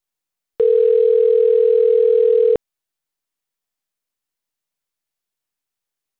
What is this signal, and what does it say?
Ringback tone (North America), 440 + 480 Hz.